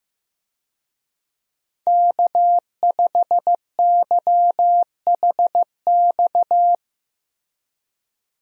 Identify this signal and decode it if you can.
Morse code, heard as a keyed tone, about 15 wpm, 700 Hz. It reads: K5YHX
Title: USA Call Signs - 5 Characters - Repeated 3x - 15wpm